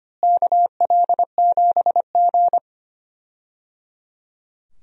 Morse code, heard as a keyed tone, about 25 wpm, 700 Hz. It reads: KL7G
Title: DXCC-100 Call Signs - 4 Characters - Repeated 3x - 25wpm